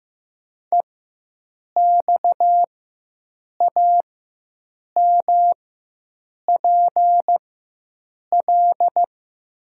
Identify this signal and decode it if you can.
Morse code, heard as a keyed tone, about 15 wpm, spaced as for 7 wpm, 700 Hz. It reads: EXAMPL
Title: 4x Character Spacing - Top 500 Words - 15wpm